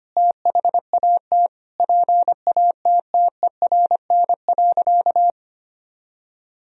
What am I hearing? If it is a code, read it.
Morse code, heard as a keyed tone, about 25 wpm, 700 Hz. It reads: THAT PATTERN.
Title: Sentences from Top 1000 Words - Review Part 2 - 25wpm